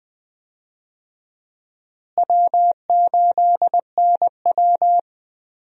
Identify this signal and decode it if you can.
Morse code, heard as a keyed tone, about 20 wpm, 700 Hz. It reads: W8NW
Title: USA Call Signs - 4 Characters - Repeated 3x - 20wpm